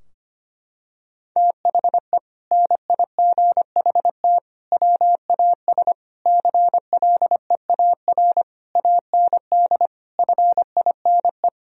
Morse code, heard as a keyed tone, about 25 wpm, 700 Hz. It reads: THE NIGHT WAS CLEAR AND FINE
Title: Sentences from Top 1000 Words - Review Part 2 - 25wpm